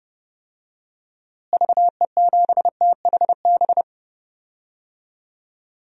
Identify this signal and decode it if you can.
Morse code, heard as a keyed tone, about 29 wpm, 700 Hz. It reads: VE7THB
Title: DXCC-100 Call Signs - 6 Characters - Repeated 2x - 30wpm